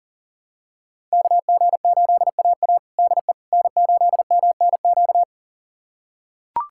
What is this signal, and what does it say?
Morse code, heard as a keyed tone, about 40 wpm, 700 Hz. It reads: KG8AA DE N8MNQ
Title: USA Callsign DE Callsign - 5 Characters - 40wpm